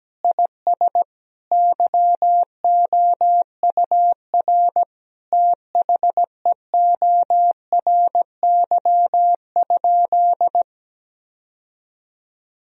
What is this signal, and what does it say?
Morse code, heard as a keyed tone, about 17 wpm, 700 Hz. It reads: IS YOUR THEORY?
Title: Sentences from Top 1000 Words - Review Part 3 - 17wpm